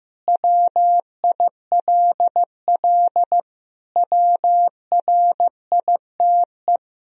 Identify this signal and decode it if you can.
Morse code, heard as a keyed tone, about 15 wpm, 700 Hz. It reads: WILL WRITE
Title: Sentences from Top 1000 Words - Review Part 1 - 15wpm